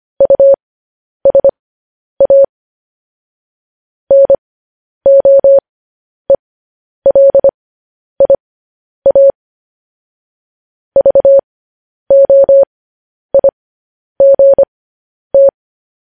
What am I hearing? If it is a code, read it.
Morse code, heard as a keyed tone, about 25 wpm, spaced as for 10 wpm, 550 Hz. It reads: USA NOELIA VOIGT